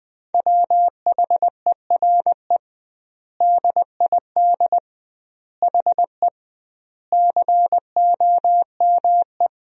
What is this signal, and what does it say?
Morse code, heard as a keyed tone, about 21 wpm, 700 Hz. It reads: WHERE DID HE COME
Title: 2x Word-Spacing - Sentences from Top 200 Words - 20wpm